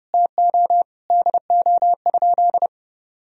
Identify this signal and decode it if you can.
Morse code, heard as a keyed tone, about 30 wpm, 700 Hz. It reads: TO DO?
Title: Sentences from Top 200 Words and Abbreviations 30wpm